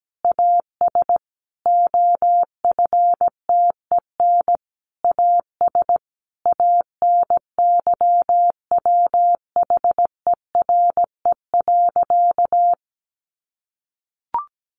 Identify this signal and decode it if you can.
Morse code, heard as a keyed tone, about 17 wpm, 700 Hz. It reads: AS OFTEN AS ANYWHERE.
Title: Sentences from Top 1400 Words 17wpm